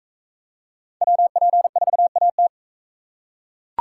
Morse code, heard as a keyed tone, about 42 wpm, 700 Hz. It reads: WP4AT